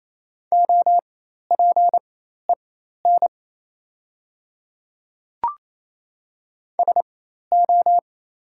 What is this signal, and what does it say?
Morse code, heard as a keyed tone, about 28 wpm, spaced as for 13 wpm, 700 Hz. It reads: OPEN SO
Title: 4x Character Spacing - Top 300 Words - 28wpm